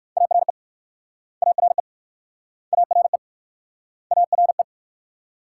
Morse code, heard as a keyed tone, about 61 wpm, 690 Hz. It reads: ARE ARE ARE ARE